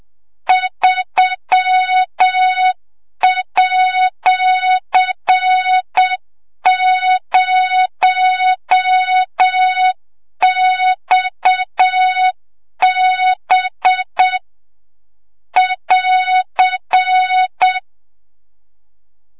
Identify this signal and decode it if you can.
Morse code, heard as a keyed tone, about 7 wpm, 3700 Hz, with noise. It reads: 3@0XB <AR>
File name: Morze-HA2MN / hang / 25 GYAKORLAT